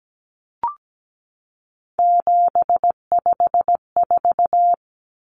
Morse code, heard as a keyed tone, about 17 wpm, 700 Hz. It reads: 754
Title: Common Cut Numbers - 3-Digit 17wpm